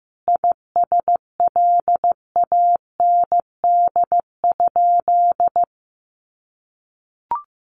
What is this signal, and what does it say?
Morse code, heard as a keyed tone, about 14 wpm, 700 Hz. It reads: ISLAND?
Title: Sentences from Top 500 Words 15wpm